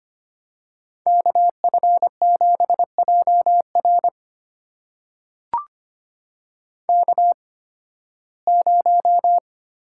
Morse code, heard as a keyed tone, about 25 wpm, spaced as for 16 wpm, 700 Hz. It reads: KF7JR K0
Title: USA Call Signs - 8x Character Spacing - 25wpm